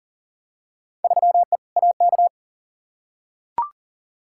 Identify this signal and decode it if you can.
Morse code, heard as a keyed tone, about 40 wpm, 700 Hz. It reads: 3E AK
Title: Field Day Exchanges Encore - 40wpm